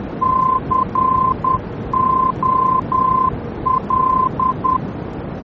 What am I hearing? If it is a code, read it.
Morse code, heard as a keyed tone, about 10 wpm, 1040 Hz, with noise. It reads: COL
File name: germany / mx / COL-D-VOR